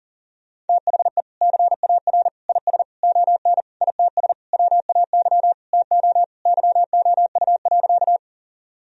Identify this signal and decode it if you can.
Morse code, heard as a keyed tone, about 40 wpm, 700 Hz. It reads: THE CAR IS ON ITS WAY TO YOU.
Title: Sentences from Top 1000 Words - Review Part 1 - 40wpm